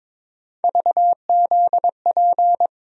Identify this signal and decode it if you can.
Morse code, heard as a keyed tone, about 22 wpm, 700 Hz. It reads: VZP
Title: USA Call Sign Postfixes - 3 Letters 22wpm